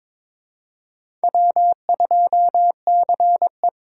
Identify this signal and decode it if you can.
Morse code, heard as a keyed tone, about 22 wpm, 700 Hz. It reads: W2CE
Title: USA Call Signs - 4 Characters - 4x Character Spacing - 22wpm